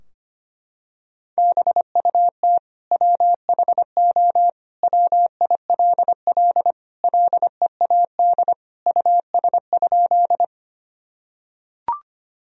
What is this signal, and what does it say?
Morse code, heard as a keyed tone, about 25 wpm, 700 Hz. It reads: BUT WHO WILL LEAD US?
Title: Sentences from Top 1000 Words - Review Part 2 - 25wpm